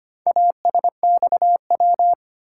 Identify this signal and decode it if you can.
Morse code, heard as a keyed tone, about 25 wpm, 700 Hz. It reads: ASXW
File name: practice / 25wpm / Four Letter-Number Combinations - 25wpm